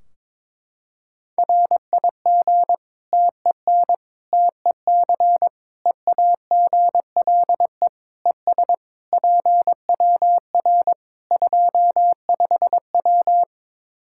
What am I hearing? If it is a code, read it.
Morse code, heard as a keyed tone, about 22 wpm, 700 Hz. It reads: RIG TEN TEC EAGLE ES PWR 25W
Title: K1USN SST QSO Practice 22wpm